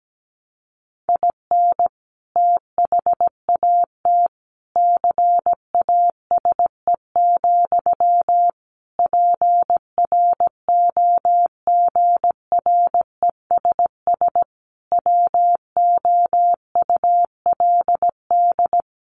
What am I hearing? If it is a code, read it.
Morse code, heard as a keyed tone, about 17 wpm, 700 Hz. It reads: IN THAT CASE, PROGRESS WOULD
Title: Sentences from Top 1400 Words 17wpm